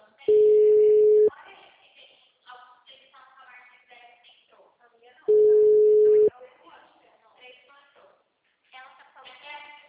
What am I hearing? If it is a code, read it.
Ringback tone (Europe, 425 Hz), with speech.
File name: out- -2003- - - .9514